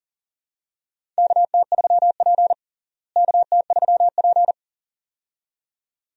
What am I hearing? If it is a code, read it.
Morse code, heard as a keyed tone, about 37 wpm, 700 Hz. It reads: KT3P KT3P
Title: USA Call Signs - 4 Characters - Repeated 2x - 40wpm